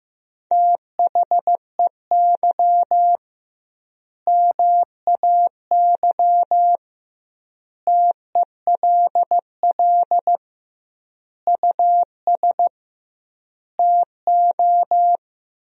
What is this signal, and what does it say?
Morse code, heard as a keyed tone, about 16 wpm, 700 Hz. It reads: THEY MAY TELL US TO
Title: 2x Word-Spacing - Sentences from Top 300 Words - 15wpm